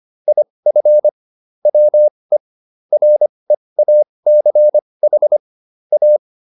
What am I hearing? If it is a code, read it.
Morse code, heard as a keyed tone, about 25 wpm, spaced as for 20 wpm, 600 Hz. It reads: IF WE REACH A